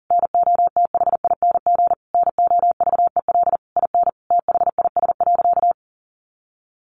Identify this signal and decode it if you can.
Morse code, heard as a keyed tone, about 40 wpm, 700 Hz. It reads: NOTHING NOVEL IN THIS.
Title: Sentences from Top 1300 Words 40wpm